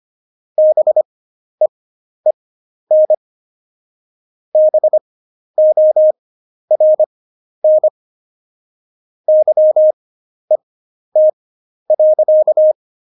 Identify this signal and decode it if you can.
Morse code, heard as a keyed tone, about 25 wpm, spaced as for 11 wpm, 620 Hz. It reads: BEEN BORN YET.
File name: Jokes-25-char-10-eff